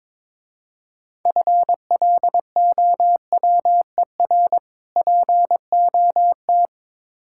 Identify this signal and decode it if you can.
Morse code, heard as a keyed tone, about 22 wpm, 690 Hz. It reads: FLOWER POT
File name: Easy Words - Speed-Racing with Hints - 22wpm